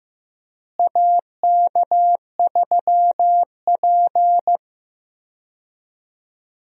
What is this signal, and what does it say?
Morse code, heard as a keyed tone, about 15 wpm, 700 Hz. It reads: AK3P
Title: USA Call Signs - 4 Characters - Repeated 2x - 15wpm